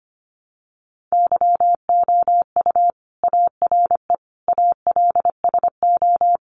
Morse code, heard as a keyed tone, about 25 wpm, 700 Hz. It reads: YOU ARE ALSO
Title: Sentences from Top 1300 Words 25wpm